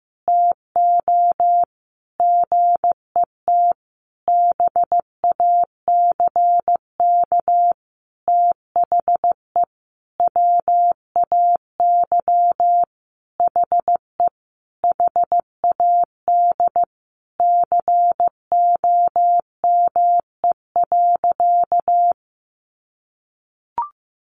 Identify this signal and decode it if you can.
Morse code, heard as a keyed tone, about 15 wpm, 700 Hz. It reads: TO GET BACK THE WAY HE HAD COME.
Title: Sentences from Top 1100 Words 15wpm